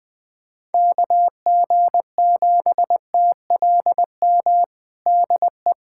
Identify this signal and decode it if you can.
Morse code, heard as a keyed tone, about 21 wpm, 700 Hz. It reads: KG7TLM DE
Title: USA Callsign DE Callsign - 6 Characters - 20wpm